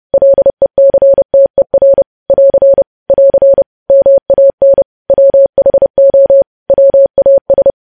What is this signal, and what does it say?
Morse code, heard as a keyed tone, about 30 wpm, 550 Hz. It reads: LECTER <AR> <AR> MAN WHO WAS
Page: news headlines in morse code at 30 wpm